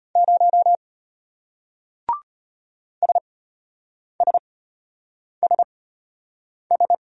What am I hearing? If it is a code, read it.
Morse code, heard as a keyed tone, about 33 wpm, spaced as for 8 wpm, 700 Hz. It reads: 0 SSSS